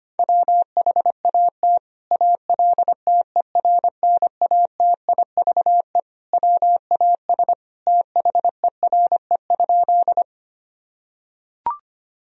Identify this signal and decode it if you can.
Morse code, heard as a keyed tone, about 25 wpm, 700 Hz. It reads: WHAT ALTERNATIVE WAS THERE?
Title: Sentences from Top 1000 Words - Review Part 4 - 25wpm